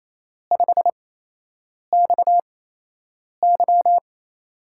Morse code, heard as a keyed tone, about 28 wpm, spaced as for 8 wpm, 700 Hz. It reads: 5XY
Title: USA Call Signs - 5 Characters - 8x Character Spacing - 28wpm